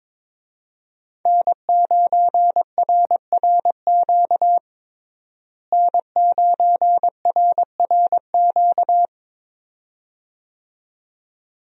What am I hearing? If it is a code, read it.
Morse code, heard as a keyed tone, about 22 wpm, 700 Hz. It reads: N9RRQ N9RRQ
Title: USA Call Signs - 5 Characters - Repeated 2x - 22wpm